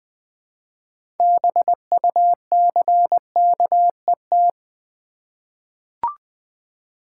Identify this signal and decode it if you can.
Morse code, heard as a keyed tone, about 20 wpm, 700 Hz. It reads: BUCKET
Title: Easy Words with Hints - 20wpm